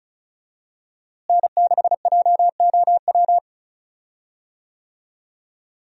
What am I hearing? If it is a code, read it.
Morse code, heard as a keyed tone, about 35 wpm, 700 Hz. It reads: N6JOW
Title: USA Call Signs - 5 Characters - Repeated 2x - 35wpm